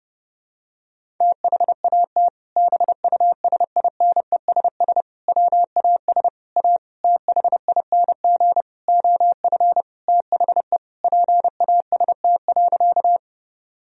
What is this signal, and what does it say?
Morse code, heard as a keyed tone, about 29 wpm, 700 Hz. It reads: THAT BUSINESS WAS A THING OF THE PAST.
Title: Sentences from Top 1000 Words - Review Part 3 - 30wpm